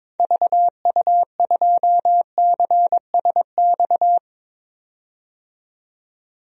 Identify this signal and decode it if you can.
Morse code, heard as a keyed tone, about 21 wpm, 700 Hz. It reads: VU2CSX
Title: DXCC-100 Call Signs - 6 Characters - 22wpm